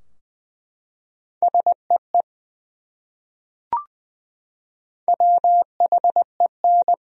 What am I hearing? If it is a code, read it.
Morse code, heard as a keyed tone, about 20 wpm, 700 Hz. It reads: SEE WHEN
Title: Top 100 Words Repeated 20wpm